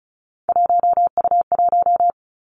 Morse code, heard as a keyed tone, about 36 wpm, 700 Hz. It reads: 1U1
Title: Three Challenging Letter-Number Combinations 35wpm